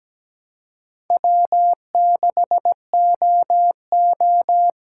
Morse code, heard as a keyed tone, about 17 wpm, 700 Hz. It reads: W6OO
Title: USA Call Signs - 4 Characters - 3x Character Spacing - 17wpm